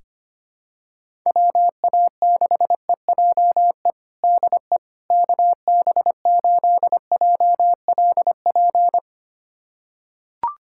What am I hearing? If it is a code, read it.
Morse code, heard as a keyed tone, about 25 wpm, 700 Hz. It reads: WA6EJE DE KB8JLP
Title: USA Callsign DE Callsign - 6 Characters - 25wpm